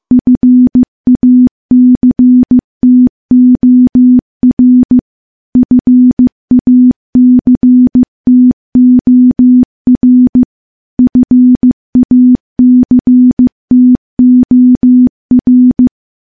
Morse code, heard as a keyed tone, about 15 wpm, 250 Hz. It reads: FACTOR FACTOR FACTOR
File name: morse-training-15WPM